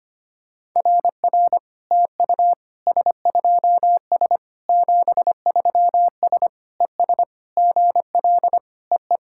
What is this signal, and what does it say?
Morse code, heard as a keyed tone, about 25 wpm, 700 Hz. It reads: RR TU S2S 73S ES GL EE
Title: SOTA QSO Practice 25wpm